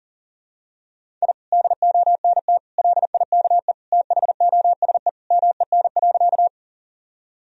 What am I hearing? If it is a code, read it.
Morse code, heard as a keyed tone, about 37 wpm, 700 Hz. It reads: I DONT LIKE THOSE MEN.
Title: Sentences from Top 1000 Words - Review Part 1 - 40wpm